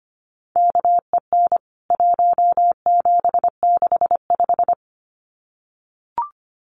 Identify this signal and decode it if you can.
Morse code, heard as a keyed tone, about 25 wpm, 700 Hz. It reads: KEN 1765
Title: CWT Exchange 25wpm